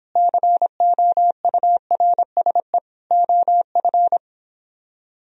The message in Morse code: COURSE OF

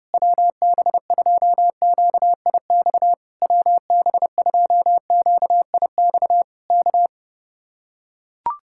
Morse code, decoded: WB2QIX WB2QIX K

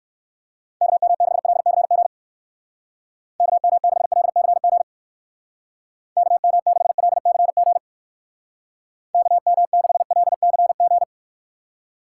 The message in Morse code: KM6LCG KM6LCG KM6LCG KM6LCG

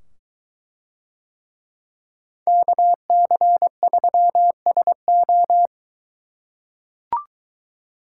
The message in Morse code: KC3SO

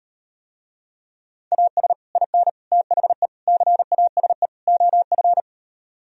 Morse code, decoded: AS IN THE CASE OF

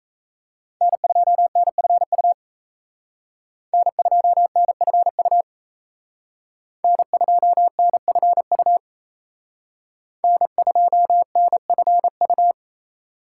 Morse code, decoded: N2NFU N2NFU N2NFU N2NFU

